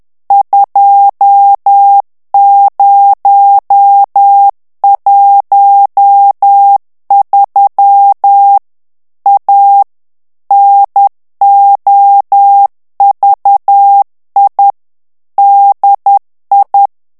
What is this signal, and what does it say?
Morse code, heard as a keyed tone, about 10 wpm, 800 Hz. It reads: 2013 A NOVI DI
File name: ITALIEN texte clair 1 vitesse 10 mots minute